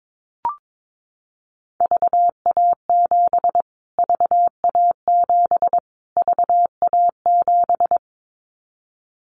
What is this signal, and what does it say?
Morse code, heard as a keyed tone, about 22 wpm, 700 Hz. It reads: VA7 VA7 VA7
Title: Callsign Prefixes for Top 25 DXCC - Repeated 3x - 22wpm